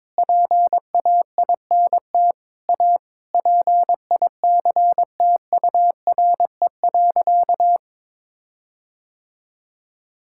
Morse code, heard as a keyed tone, about 22 wpm, 700 Hz. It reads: PAINT A PICTURE.
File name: practice / 22wpm / Sentences with 8-Word Phrases - 22wpm